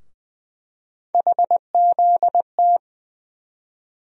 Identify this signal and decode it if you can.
Morse code, heard as a keyed tone, about 20 wpm, 700 Hz. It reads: HZT
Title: USA Call Sign Postfixes - 3 Letters 20wpm